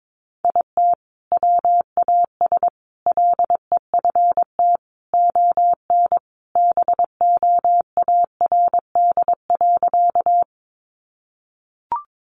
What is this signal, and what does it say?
Morse code, heard as a keyed tone, about 22 wpm, 700 Hz. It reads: IT WAS LEFT ON BOARD.
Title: Sentences from Top 700 Words 22wpm